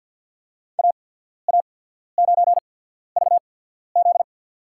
Morse code, heard as a keyed tone, about 54 wpm, spaced as for 14 wpm, 700 Hz. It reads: AA9VZ
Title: USA Call Signs - 5 Characters - 8x Character Spacing - 50wpm